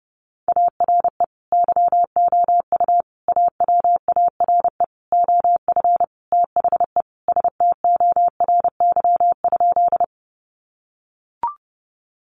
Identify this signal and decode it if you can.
Morse code, heard as a keyed tone, about 30 wpm, 700 Hz. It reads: ARE YOU AWARE OF THE STORY?
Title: Sentences from Top 1200 Words 30wpm